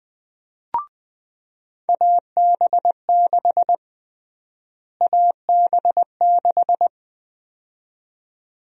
Morse code, heard as a keyed tone, about 20 wpm, 700 Hz. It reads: AB6 AB6
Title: USA Call Sign Prefixes - Repeated 2x - 20wpm